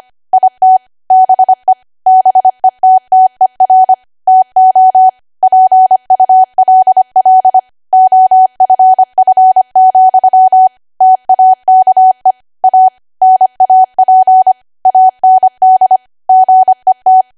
Morse code, heard as a keyed tone, about 25 wpm, 750 Hz. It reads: IT BE BETTER TO PULL OFF, TAKE A NAP AND GET
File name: cwa / W1AW-Level3 / 25WPM